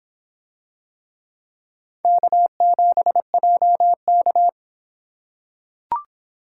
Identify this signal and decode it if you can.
Morse code, heard as a keyed tone, about 26 wpm, 700 Hz. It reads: K7JK